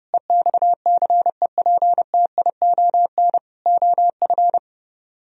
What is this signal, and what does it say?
Morse code, heard as a keyed tone, about 33 wpm, 700 Hz. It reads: EXCEPTION OF